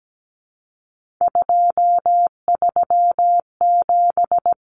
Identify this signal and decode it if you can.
Morse code, heard as a keyed tone, about 17 wpm, 700 Hz. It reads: 237
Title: Numbers - 3-Digit 17wpm